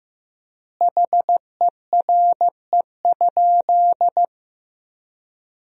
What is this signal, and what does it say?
Morse code, heard as a keyed tone, about 14 wpm, 700 Hz. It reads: HERE?
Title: 4x Word-Spacing - Sentences from Top 400 Words - 15wpm